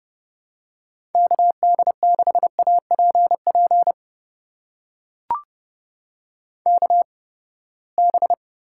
Morse code, heard as a keyed tone, about 30 wpm, spaced as for 21 wpm, 700 Hz. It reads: KD6APP KB
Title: USA Call Signs - 6 Characters - 8x Character Spacing - 30wpm